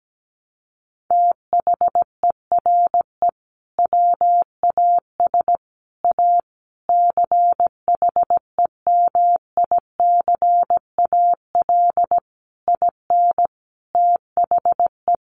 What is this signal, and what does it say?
Morse code, heard as a keyed tone, about 17 wpm, 700 Hz. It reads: THERE WAS A CHEMICAL IN THE
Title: Sentences from Top 1200 Words 17wpm